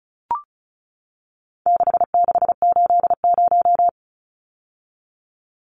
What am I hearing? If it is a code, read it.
Morse code, heard as a keyed tone, about 34 wpm, 700 Hz. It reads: 6680
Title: Numbers - 4-Digit 35wpm